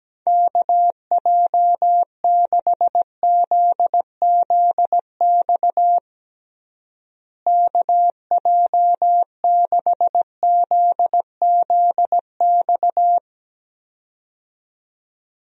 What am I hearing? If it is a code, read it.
Morse code, heard as a keyed tone, about 17 wpm, 700 Hz. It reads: KJ6ZZX KJ6ZZX
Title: USA Call Signs - 6 Characters - Repeated 2x - 17wpm